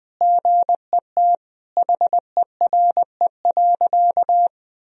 Morse code, heard as a keyed tone, about 19 wpm, 700 Hz. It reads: GET HERE.
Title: Sentences from Top 1000 Words - Review Part 2 - 20wpm